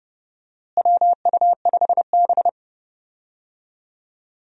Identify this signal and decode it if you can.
Morse code, heard as a keyed tone, about 32 wpm, 700 Hz. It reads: WU5B
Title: USA Call Signs - 4 Characters - Repeated 2x - 30wpm